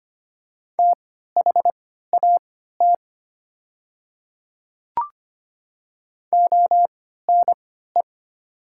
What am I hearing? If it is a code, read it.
Morse code, heard as a keyed tone, about 25 wpm, spaced as for 14 wpm, 690 Hz. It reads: THAT ONE
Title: 3x Character Spacing - Top 200 Words - 25wpm